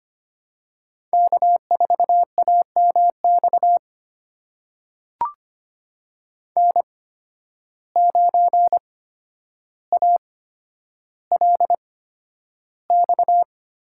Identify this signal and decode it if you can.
Morse code, heard as a keyed tone, about 25 wpm, spaced as for 11 wpm, 700 Hz. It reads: K4AMX N9ALX
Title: USA Call Signs - 5 Characters - 8x Character Spacing - 25wpm